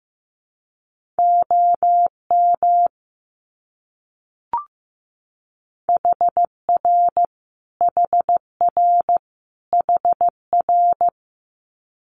Morse code, heard as a keyed tone, about 15 wpm, 700 Hz. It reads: OM HR HR HR
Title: Abbreviations Repeated 3x - 15wpm